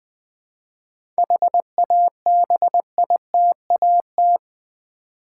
Morse code, heard as a keyed tone, about 20 wpm, 700 Hz. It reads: HABITAT
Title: 7 Letter Words - First 3 Letters Given - 20wpm